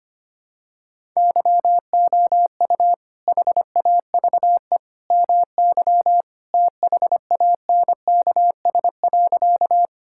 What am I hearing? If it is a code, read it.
Morse code, heard as a keyed tone, about 25 wpm, 700 Hz. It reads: YOU HAVE MY THANKS.
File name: practice / 25wpm / Sentences with 2-Word Phrases - 25wpm